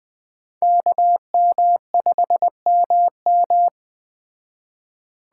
Morse code, heard as a keyed tone, about 20 wpm, 700 Hz. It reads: KM5MM
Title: Speed Racing - USA Call Signs 17wpm